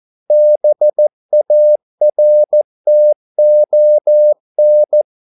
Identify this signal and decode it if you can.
Morse code, heard as a keyed tone, about 14 wpm, 600 Hz. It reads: BARTON
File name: mp3 files / a dogs mistake - 14wpm